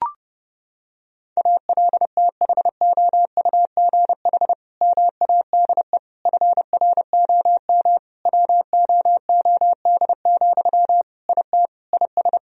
Morse code, heard as a keyed tone, about 30 wpm, 700 Hz. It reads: ALTHOUGH MADE FROM WOOD, IT IS
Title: Sentences from Top 1000 Words - Review Part 3 - 30wpm